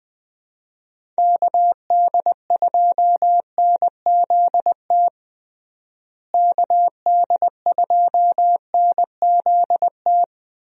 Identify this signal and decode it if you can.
Morse code, heard as a keyed tone, about 20 wpm, 700 Hz. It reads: KD2NZT KD2NZT
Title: USA Call Signs - 6 Characters - Repeated 2x - 20wpm